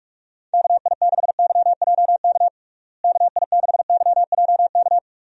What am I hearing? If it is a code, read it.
Morse code, heard as a keyed tone, about 41 wpm, 700 Hz. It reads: KI6YJK KI6YJK
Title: USA Call Signs - 6 Characters - Repeated 3x - 45wpm